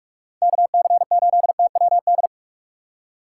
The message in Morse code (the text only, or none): KC8TWD